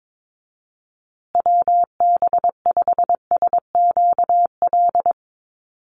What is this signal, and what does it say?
Morse code, heard as a keyed tone, about 22 wpm, 700 Hz. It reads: WB5SQL
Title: USA Call Signs Repeated 3x 22wpm